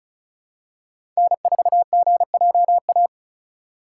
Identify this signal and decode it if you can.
Morse code, heard as a keyed tone, about 36 wpm, 700 Hz. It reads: N4GJA
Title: USA Call Signs - 5 Characters - 3x Character Spacing - 35wpm